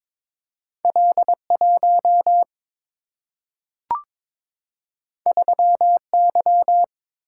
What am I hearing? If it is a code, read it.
Morse code, heard as a keyed tone, about 23 wpm, 690 Hz. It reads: L1 3Y
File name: practice / 22wpm / Two Letter-Number Combinations - 22wpm